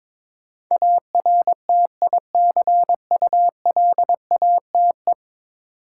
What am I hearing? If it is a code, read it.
Morse code, heard as a keyed tone, about 22 wpm, 700 Hz. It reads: ARTICULATE
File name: practice / 22wpm / 10 Letter Words - First 3 Letters Given - 22wpm